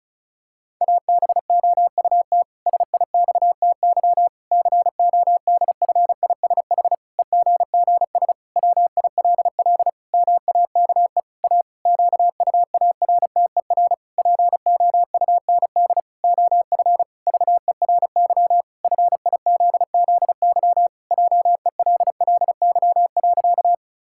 Morse code, heard as a keyed tone, about 34 wpm, 700 Hz. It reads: ABOUT SIXTY CODFISH EGGS WILL MAKE A QUARTER POUND OF VERY FIZZY JELLY.
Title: Pangram Challenge v2 - 35wpm